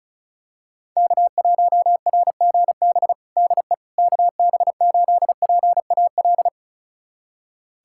Morse code, heard as a keyed tone, about 36 wpm, 700 Hz. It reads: K1RGB DE KB8PAL
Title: USA Callsign DE Callsign 35wpm